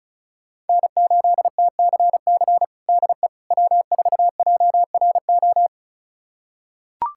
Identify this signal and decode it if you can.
Morse code, heard as a keyed tone, about 36 wpm, 700 Hz. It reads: N8TCC DE W4JRO
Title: USA Callsign DE Callsign - 5 Characters - 35wpm